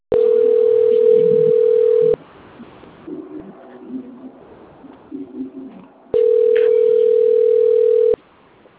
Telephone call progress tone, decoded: ringback tone

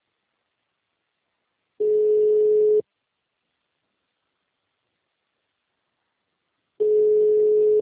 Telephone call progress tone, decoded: ringback tone

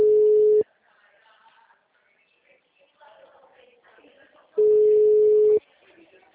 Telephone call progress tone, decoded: ringback tone

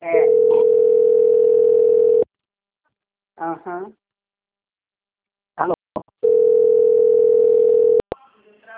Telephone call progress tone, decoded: ringback tone

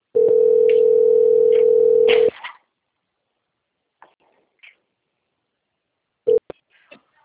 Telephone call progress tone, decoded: ringback tone

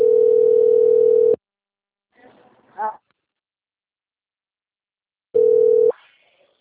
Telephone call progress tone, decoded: ringback tone

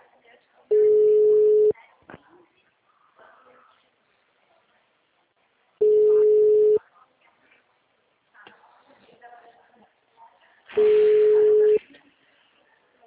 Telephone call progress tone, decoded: ringback tone

